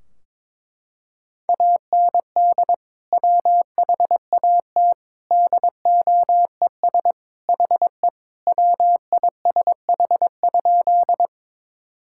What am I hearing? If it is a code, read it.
Morse code, heard as a keyed tone, about 22 wpm, 700 Hz. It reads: AND WHAT DOES HE WISH?
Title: Sentences from Top 1600 Words 22wpm